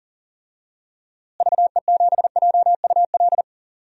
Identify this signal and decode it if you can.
Morse code, heard as a keyed tone, about 40 wpm, 700 Hz. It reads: VE7JUL
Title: DXCC-100 Call Signs - 6 Characters - Repeated 3x - 40wpm